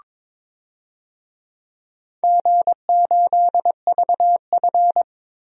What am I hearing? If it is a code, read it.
Morse code, heard as a keyed tone, about 22 wpm, 700 Hz. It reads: G8VF
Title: DXCC-100 Call Signs - 4 Characters - Repeated 2x - 22wpm